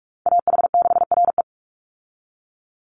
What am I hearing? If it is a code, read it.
Morse code, heard as a keyed tone, about 44 wpm, 700 Hz. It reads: AH6RE
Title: Call Signs Repeated 3x - 45wpm